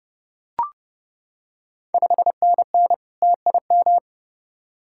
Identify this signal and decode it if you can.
Morse code, heard as a keyed tone, about 32 wpm, 700 Hz. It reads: 5NN TIM
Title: Ontario QSO Party Exchange 30wpm